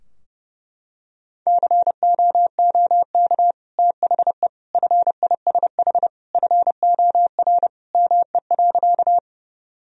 Morse code, heard as a keyed tone, about 30 wpm, 700 Hz. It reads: COOK THE FISH FOR ME.
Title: Sentences from Top 1000 Words - Review Part 4 - 30wpm